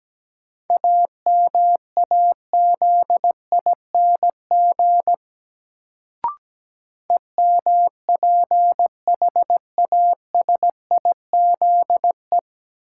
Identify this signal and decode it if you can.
Morse code, heard as a keyed tone, about 17 wpm, 700 Hz. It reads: AMAZING EMPHASIZE
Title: Sentences from Top 1500 Words - New Word Drill Alt - 17wpm